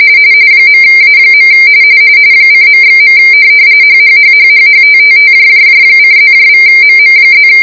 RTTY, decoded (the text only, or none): .A. :./. HHHHHHTTTJJJ:+=MM+